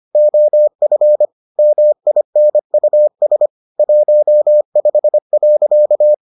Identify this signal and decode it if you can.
Morse code, heard as a keyed tone, about 25 wpm, 600 Hz. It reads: OF MINUS 15.